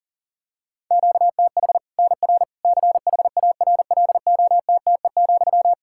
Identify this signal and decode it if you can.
Morse code, heard as a keyed tone, about 40 wpm, 700 Hz. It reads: QTH NR CHARLOTTE,
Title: Practice QSO - Exchange 1 - Beginning - 40wpm